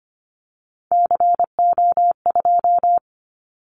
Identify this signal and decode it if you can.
Morse code, heard as a keyed tone, about 25 wpm, 700 Hz. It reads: CO2
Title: Callsign Prefixes for Top 100 DXCC - 25wpm